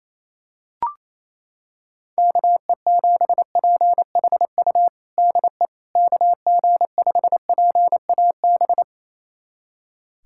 Morse code, heard as a keyed tone, about 27 wpm, 700 Hz. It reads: KE7PHU DE KG5PAB
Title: USA Callsign DE Callsign - 6 Characters - 28wpm